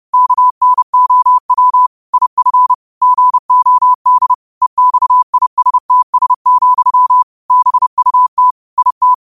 Morse code, heard as a keyed tone, about 30 wpm, 1000 Hz. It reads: KNOW IF GOD EXISTS, BUT IT